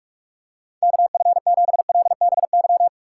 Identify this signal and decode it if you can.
Morse code, heard as a keyed tone, about 46 wpm, 700 Hz. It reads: KF7LBY
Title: USA Call Signs - 6 Characters - 2x Character Spacing - 45wpm